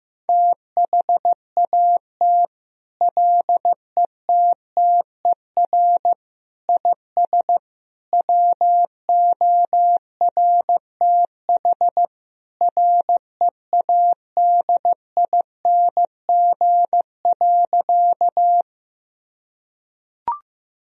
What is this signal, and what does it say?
Morse code, heard as a keyed tone, about 15 wpm, 700 Hz. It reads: THAT LETTER IS WORTH READING.
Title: Sentences from Top 1000 Words - Review Part 3 - 15wpm